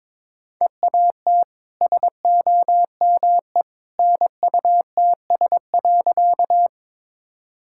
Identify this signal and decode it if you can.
Morse code, heard as a keyed tone, about 22 wpm, 690 Hz. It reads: EAT SOME NUTS.
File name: practice / 22wpm / Easy Sentences with Hints - 22wpm